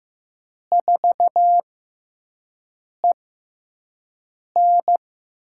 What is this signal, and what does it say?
Morse code, heard as a keyed tone, about 15 wpm, spaced as for 5 wpm, 700 Hz. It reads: 4EN
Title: USA Call Signs - 6x Character Spacing - 15wpm